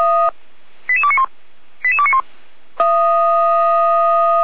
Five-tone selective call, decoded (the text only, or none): 88291 88291